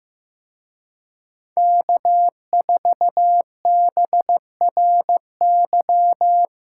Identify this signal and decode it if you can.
Morse code, heard as a keyed tone, about 16 wpm, 700 Hz. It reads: K4BRY
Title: Speed Racing - USA Call Signs 15wpm